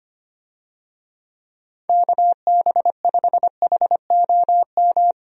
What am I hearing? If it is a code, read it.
Morse code, heard as a keyed tone, about 25 wpm, 700 Hz. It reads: KB5HOM
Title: USA Call Signs - 6 Characters - Repeated 3x - 25wpm